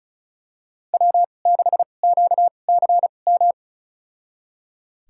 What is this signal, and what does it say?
Morse code, heard as a keyed tone, about 34 wpm, spaced as for 25 wpm, 700 Hz. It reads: W6QCM
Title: USA Call Signs - 2x Character Spacing - 35wpm